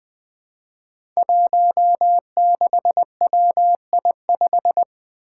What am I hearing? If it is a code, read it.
Morse code, heard as a keyed tone, about 20 wpm, 700 Hz. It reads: 16WI5
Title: Five Challenging Letter-Number Combinations 20wpm